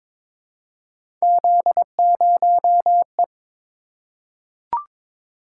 Morse code, heard as a keyed tone, about 23 wpm, 700 Hz. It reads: Z0E
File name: practice / 22wpm / Three Letter-Number Combinations - 22wpm